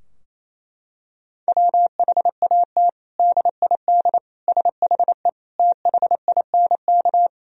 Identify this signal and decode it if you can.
Morse code, heard as a keyed tone, about 27 wpm, 700 Hz. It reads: WHAT DID SHE THINK
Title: Sentences from Top 1000 Words - Review Part 4 - 28wpm